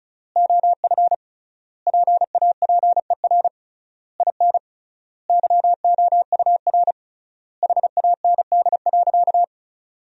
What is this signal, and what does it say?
Morse code, heard as a keyed tone, about 36 wpm, 700 Hz. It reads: OF PAPER IN YOUR HAND.
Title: 3x Word-Spacing - Sentences from Top 400 Words - 35wpm